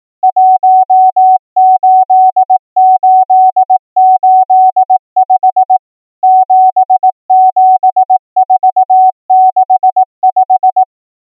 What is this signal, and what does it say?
Morse code, heard as a keyed tone, about 18 wpm, 750 Hz. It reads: 18885 77465
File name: numbers / V2-morse